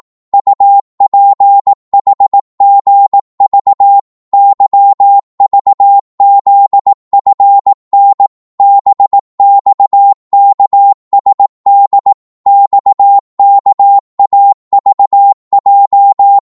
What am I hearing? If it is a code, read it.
Morse code, heard as a keyed tone, about 18 wpm, 800 Hz. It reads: UPHGV YVZFN BXKSD XKAVJ